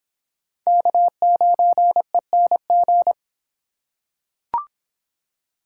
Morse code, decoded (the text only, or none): K9ENG